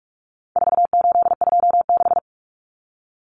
Morse code, heard as a keyed tone, about 44 wpm, 700 Hz. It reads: 4826
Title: Numbers - 4-Digit 45wpm